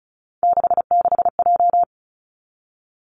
Morse code, heard as a keyed tone, about 36 wpm, 700 Hz. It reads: 66J
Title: Three Challenging Letter-Number Combinations 35wpm